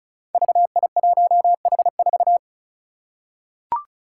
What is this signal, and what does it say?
Morse code, heard as a keyed tone, about 36 wpm, 700 Hz. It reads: VI1H4
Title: Five Challenging Letter-Number Combinations 35wpm